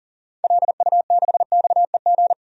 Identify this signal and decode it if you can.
Morse code, heard as a keyed tone, about 40 wpm, 700 Hz. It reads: LU6XEG
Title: DXCC-100 Call Signs - 6 Characters - 40wpm